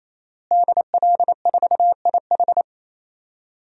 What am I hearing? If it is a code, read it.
Morse code, heard as a keyed tone, about 27 wpm, 700 Hz. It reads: DL4IH
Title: DXCC-100 Call Signs - 5 Characters - Repeated 3x - 28wpm